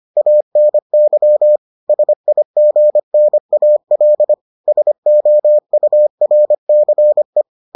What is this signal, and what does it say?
Morse code, heard as a keyed tone, about 25 wpm, 600 Hz. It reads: ANY SIGNAL SOURCE